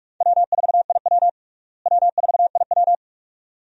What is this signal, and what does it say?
Morse code, heard as a keyed tone, about 46 wpm, 700 Hz. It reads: W4IW W4IW
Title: USA Call Signs - 4 Characters - Repeated 2x - 45wpm